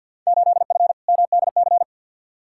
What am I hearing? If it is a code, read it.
Morse code, heard as a keyed tone, about 52 wpm, 700 Hz. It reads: 8F MDC
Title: Field Day Exchanges Encore - 50wpm